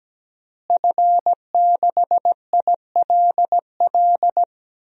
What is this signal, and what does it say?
Morse code, heard as a keyed tone, about 17 wpm, 700 Hz. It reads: F6ILL
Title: DXCC-100 Call Signs - 5 Characters - Repeated 2x - 17wpm